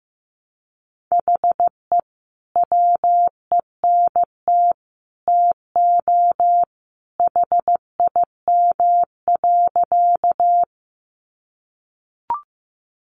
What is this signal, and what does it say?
Morse code, heard as a keyed tone, about 15 wpm, 700 Hz. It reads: HE WENT TO HIM.
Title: Sentences from Top 200 Words 15wpm